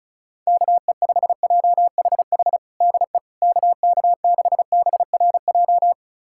Morse code, heard as a keyed tone, about 34 wpm, 700 Hz. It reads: KE5JHH DE KK6BRJ
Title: USA Callsign DE Callsign - 6 Characters - 35wpm